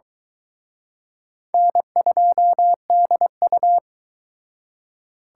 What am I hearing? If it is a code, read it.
Morse code, heard as a keyed tone, about 23 wpm, 700 Hz. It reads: N2DU